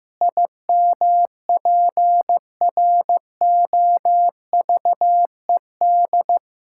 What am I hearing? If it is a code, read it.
Morse code, heard as a keyed tone, about 15 wpm, 700 Hz. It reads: IMPROVED
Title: Pangram Challenge v2 - 15wpm